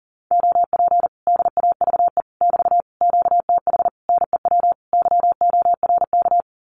Morse code, heard as a keyed tone, about 40 wpm, 700 Hz. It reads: OP DAVE <BT> QTH NEW YORK
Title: Common QSO Phrases Encore 40wpm